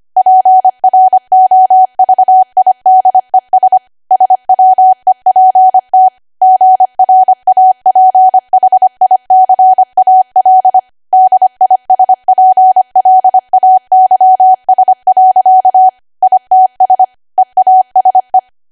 Morse code, heard as a keyed tone, about 25 wpm, 750 Hz. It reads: PROVIDES SWEPT GRAPHICAL DISPLAYS. ITS EASE
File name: Archive / 25 WPM / 25WPM